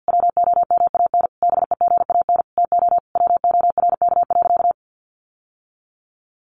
Morse code, heard as a keyed tone, about 48 wpm, 700 Hz. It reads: WOMAN BEGAN TO WORK.
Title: Sentences from Top 1100 Words 50wpm